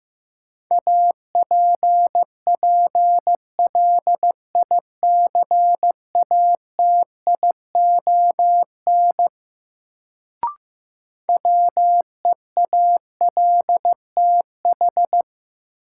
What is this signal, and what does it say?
Morse code, heard as a keyed tone, about 15 wpm, 700 Hz. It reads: APPLICATION WEALTH
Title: Sentences from Top 1200 Words - New Word Drill Alt - 15wpm